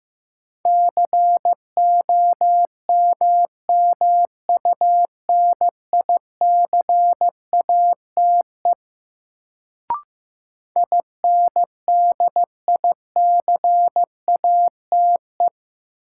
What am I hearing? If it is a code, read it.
Morse code, heard as a keyed tone, about 15 wpm, 700 Hz. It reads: COMMUNICATE INDICATE
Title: Sentences from Top 1300 Words - New Word Drill - 15wpm